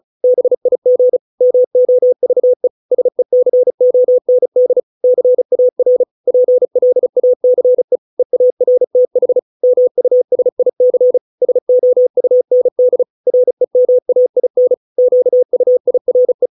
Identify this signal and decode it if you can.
Morse code, heard as a keyed tone, about 34 wpm, 500 Hz. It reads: BIG MOVE SECOND CAR PLACE EARTH MUSIC SOUND REMAIN QUIRE